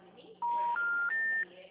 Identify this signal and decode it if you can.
Special information tone, with speech.
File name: out- -2002- - - .93737